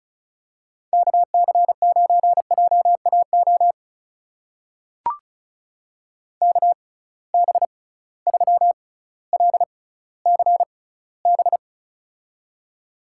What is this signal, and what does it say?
Morse code, heard as a keyed tone, about 34 wpm, spaced as for 18 wpm, 700 Hz. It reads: KC9JAO KB3LCB
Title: USA Call Signs - 6 Characters - 6x Character Spacing - 35wpm